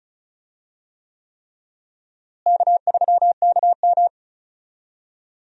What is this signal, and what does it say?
Morse code, heard as a keyed tone, about 36 wpm, 700 Hz. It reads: K3KM
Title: USA Call Signs - 4 Characters - Repeated 3x - 35wpm